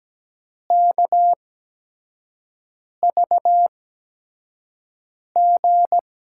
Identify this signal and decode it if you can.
Morse code, heard as a keyed tone, about 17 wpm, spaced as for 5 wpm, 700 Hz. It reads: KVG
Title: USA Call Signs - 5 Characters - 8x Character Spacing - 17wpm